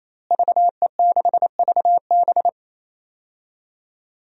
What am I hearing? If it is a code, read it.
Morse code, heard as a keyed tone, about 28 wpm, 700 Hz. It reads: VE6VB
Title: DXCC-100 Call Signs - 5 Characters - Repeated 3x - 28wpm